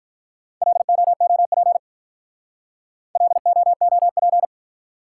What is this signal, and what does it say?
Morse code, heard as a keyed tone, about 46 wpm, 700 Hz. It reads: LOOP LOOP